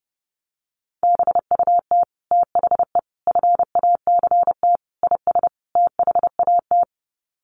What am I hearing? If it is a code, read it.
Morse code, heard as a keyed tone, about 30 wpm, 700 Hz. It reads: BUT THE FACT IS THAT
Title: Sets of 5 Words Encore2 30wpm